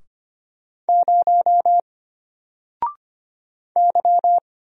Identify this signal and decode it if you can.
Morse code, heard as a keyed tone, about 25 wpm, 700 Hz. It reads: 0 Y
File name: Single-Letter-Number-25wpm